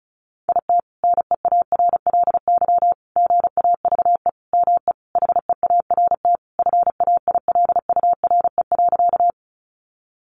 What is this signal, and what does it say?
Morse code, heard as a keyed tone, about 34 wpm, 700 Hz. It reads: IT NEARLY GAVE ME HEART FAILURE.
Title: Sentences from Top 1300 Words 35wpm